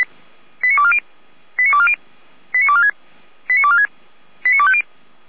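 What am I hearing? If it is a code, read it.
Five-tone selective call (ZVEI), heard as quick runs of five tones, tones of about 70 ms. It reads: 89240 89240 89246 89246 89250
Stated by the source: ZVEI five-tone call received on SDR, from audio.f85360.feuerwehr-orlamuende.org fire brigade server.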